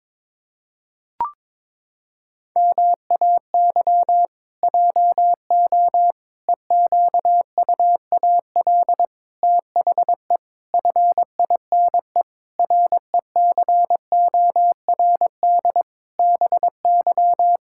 Morse code, heard as a keyed tone, about 22 wpm, 700 Hz. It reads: MAY JO EQUAL THE FINE RECORD BY
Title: Pangram Challenge v2 - 22wpm